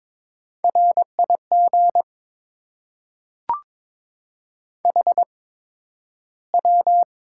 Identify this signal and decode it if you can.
Morse code, heard as a keyed tone, about 22 wpm, spaced as for 12 wpm, 700 Hz. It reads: RIG HW
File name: Top 20 QSO Elements - 8x Character Spacing - 22wpm